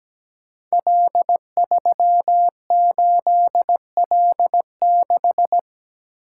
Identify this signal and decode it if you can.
Morse code, heard as a keyed tone, about 17 wpm, 700 Hz. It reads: L38L6
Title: Code Groups - 5 Letter:Numbers - 17wpm